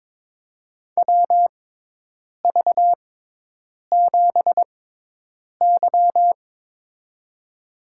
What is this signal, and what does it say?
Morse code, heard as a keyed tone, about 21 wpm, spaced as for 8 wpm, 700 Hz. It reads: WV7Y
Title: USA Call Signs - 6x Character Spacing - 22wpm